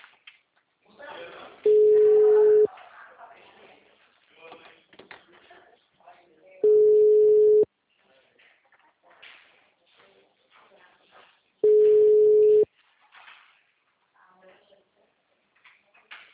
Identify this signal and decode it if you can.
Ringback tone (Europe, 425 Hz), with speech.